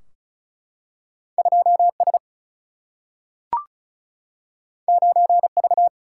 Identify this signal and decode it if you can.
Morse code, heard as a keyed tone, about 36 wpm, 700 Hz. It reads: 2S 9V
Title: Single-Letter Single-Number 35wpm